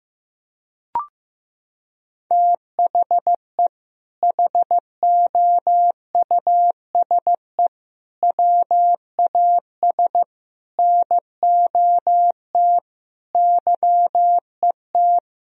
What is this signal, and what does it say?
Morse code, heard as a keyed tone, about 16 wpm, 700 Hz. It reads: THE HOUSE WAS NOT YET
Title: Sentences from Top 1000 Words - Review Part 2 - 15wpm